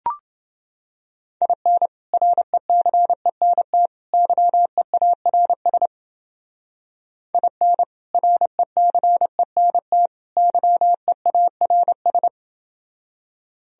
Morse code, heard as a keyed tone, about 28 wpm, 700 Hz. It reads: IN RECENT YEARS IN RECENT YEARS